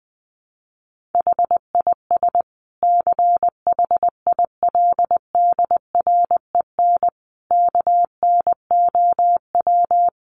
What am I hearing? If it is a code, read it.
Morse code, heard as a keyed tone, about 20 wpm, 700 Hz. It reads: HIS CHILDREN KNOW
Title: Sentences from Top 900 Words 20wpm